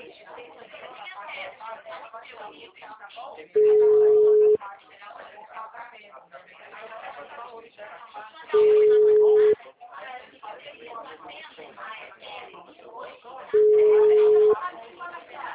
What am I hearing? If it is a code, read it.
Ringback tone (Europe, 425 Hz), with speech.